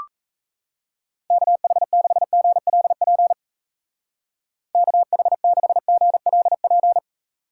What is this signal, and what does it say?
Morse code, heard as a keyed tone, about 35 wpm, 700 Hz. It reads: KH6GLP KH6GLP
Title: Speed Racing - USA 2x3 Call Signs 28wpm